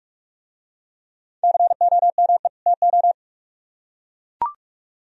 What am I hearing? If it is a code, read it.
Morse code, heard as a keyed tone, about 44 wpm, 700 Hz. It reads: COME TO